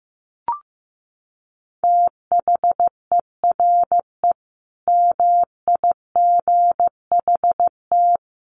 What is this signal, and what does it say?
Morse code, heard as a keyed tone, about 15 wpm, 700 Hz. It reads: THERE MIGHT
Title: Sentences from Top 1100 Words 15wpm